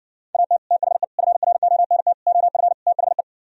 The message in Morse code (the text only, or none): AT THE FRONT OF THE